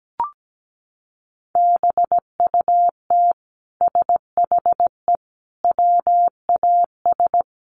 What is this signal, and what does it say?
Morse code, heard as a keyed tone, about 17 wpm, 700 Hz. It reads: BUT SHE WAS
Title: Sentences from Top 1000 Words 17wpm